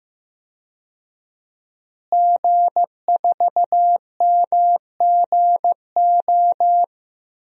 Morse code, heard as a keyed tone, about 15 wpm, 700 Hz. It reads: G4MGO
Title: DXCC-100 Call Signs - 5 Characters - Repeated 2x - 15wpm